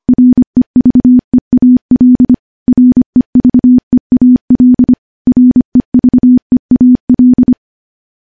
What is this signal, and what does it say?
Morse code, heard as a keyed tone, about 25 wpm, 250 Hz. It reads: REVEAL REVEAL REVEAL
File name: morse / morse-training-25WPM